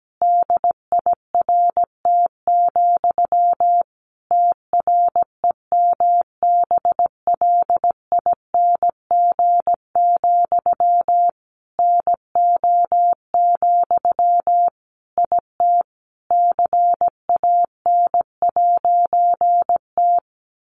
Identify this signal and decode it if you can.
Morse code, heard as a keyed tone, about 17 wpm, 700 Hz. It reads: DIRT, TREMBLING, NO, IT CAN'T
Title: A Sound Of Thunder 17wpm